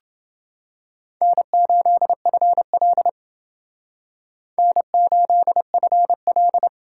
Morse code, heard as a keyed tone, about 29 wpm, 700 Hz. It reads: N8FL N8FL